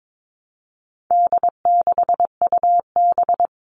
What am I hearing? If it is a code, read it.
Morse code, heard as a keyed tone, about 22 wpm, 700 Hz. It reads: D6UB
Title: All Cut Numbers - 4-Digit 22wpm